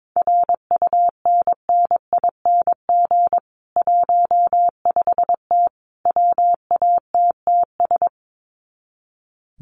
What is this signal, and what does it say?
Morse code, heard as a keyed tone, about 22 wpm, 700 Hz. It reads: RUNNING 15T WATTS
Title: Common QSO Phrases 22wpm